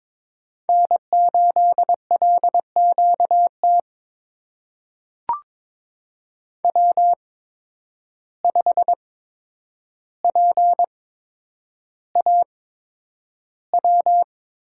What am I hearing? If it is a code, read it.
Morse code, heard as a keyed tone, about 22 wpm, spaced as for 9 wpm, 700 Hz. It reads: N8LQT W5PAW
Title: USA Call Signs - 8x Character Spacing - 22wpm